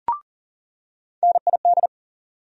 Morse code, heard as a keyed tone, about 40 wpm, 700 Hz. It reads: NID
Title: USA Call Sign Postfixes - 3 Letters 40wpm